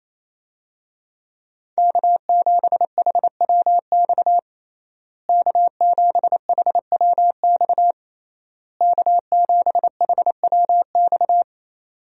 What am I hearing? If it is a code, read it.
Morse code, heard as a keyed tone, about 28 wpm, 700 Hz. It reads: K7HWX K7HWX K7HWX
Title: USA Call Signs - 5 Characters - Repeated 3x - 28wpm